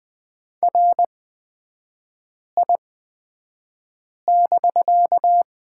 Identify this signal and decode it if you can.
Morse code, heard as a keyed tone, about 20 wpm, spaced as for 5 wpm, 700 Hz. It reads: RI<BK>
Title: 250 Lesson 40e - BK - Three Characters 20wpm